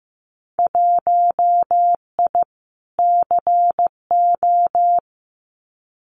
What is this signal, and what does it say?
Morse code, heard as a keyed tone, about 15 wpm, 700 Hz. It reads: 1I CO
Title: Winter Field Day Exchanges Encore 15wpm